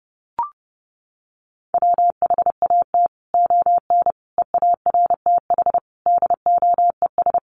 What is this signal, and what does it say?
Morse code, heard as a keyed tone, about 30 wpm, 700 Hz. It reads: WHAT ON EARTH DOES
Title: Sentences from Top 200 Words 30wpm